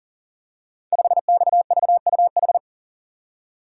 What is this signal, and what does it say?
Morse code, heard as a keyed tone, about 42 wpm, 690 Hz. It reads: 5XVUH
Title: Code Groups - 5 Letter:Numbers - 40wpm